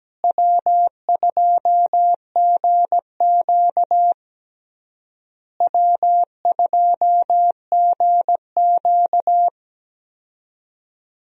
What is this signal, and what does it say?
Morse code, heard as a keyed tone, about 17 wpm, 700 Hz. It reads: W2GQ W2GQ
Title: USA Call Signs - 4 Characters - Repeated 3x - 17wpm